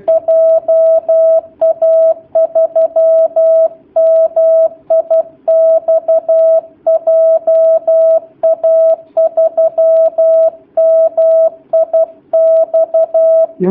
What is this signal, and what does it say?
Morse code, heard as a keyed tone, about 12 wpm, 650 Hz. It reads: JA3MIX JA3MIX
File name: jh8cbh / 2019cw / sp11- w